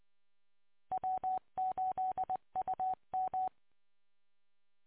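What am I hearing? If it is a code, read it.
Morse code, heard as a keyed tone, about 22 wpm, 750 Hz. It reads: W8UM
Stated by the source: Morse recorded on 145.230.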